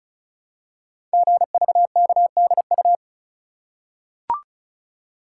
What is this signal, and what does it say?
Morse code, heard as a keyed tone, about 36 wpm, 700 Hz. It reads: GVKDU
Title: Code Groups - 5 Letter:Numbers - 35wpm